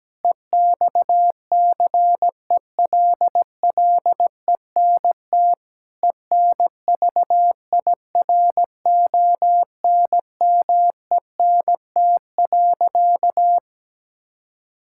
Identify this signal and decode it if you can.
Morse code, heard as a keyed tone, about 17 wpm, 700 Hz. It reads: EXCELLENT ENVIRONMENT.
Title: Sentences from Top 1000 Words - Review Part 5 - 17wpm